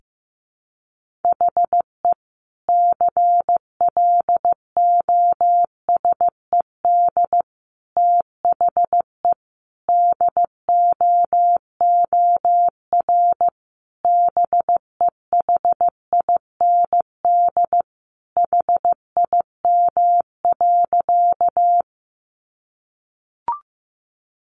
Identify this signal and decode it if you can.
Morse code, heard as a keyed tone, about 15 wpm, 700 Hz. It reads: HE CLOSED THE DOOR BEHIND HIM.
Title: Sentences from Top 1400 Words 15wpm